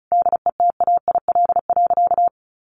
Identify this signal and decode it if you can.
Morse code, heard as a keyed tone, about 34 wpm, 700 Hz. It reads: DETAIL.
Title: Sentences from Top 1500 Words 35wpm